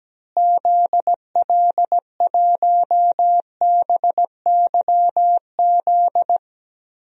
Morse code, heard as a keyed tone, about 17 wpm, 700 Hz. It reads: ZL1BYZ
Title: DXCC-100 Call Signs - 6 Characters - Repeated 2x - 17wpm